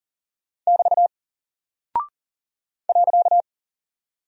Morse code, heard as a keyed tone, about 40 wpm, 700 Hz. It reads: <BT> .
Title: Single Character (Letter:Number:Punctuation:Prosign) Alt 40wpm